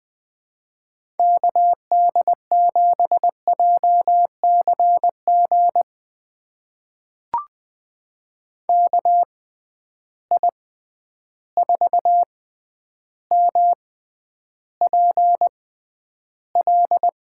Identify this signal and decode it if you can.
Morse code, heard as a keyed tone, about 20 wpm, spaced as for 10 wpm, 700 Hz. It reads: KD7JCG KI4MPL
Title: USA Call Signs - 6x Character Spacing - 20wpm